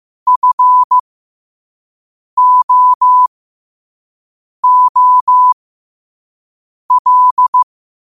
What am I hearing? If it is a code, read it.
Morse code, heard as a keyed tone, about 15 wpm, spaced as for 5 wpm, 1000 Hz. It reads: FOOL
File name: qotdfiles / 5wpm